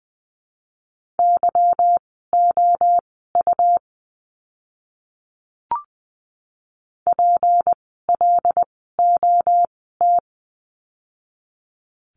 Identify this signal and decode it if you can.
Morse code, heard as a keyed tone, about 20 wpm, spaced as for 15 wpm, 700 Hz. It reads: YOU PLOT
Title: 153 Lesson 25f - Letter p - Words 20wpm